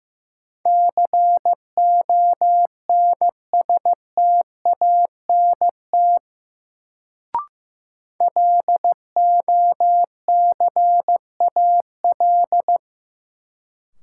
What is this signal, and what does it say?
Morse code, heard as a keyed tone, about 15 wpm, 700 Hz. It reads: CONSTANT LOCAL
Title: Sentences from Top 1500 Words - New Word Drill Alt - 15wpm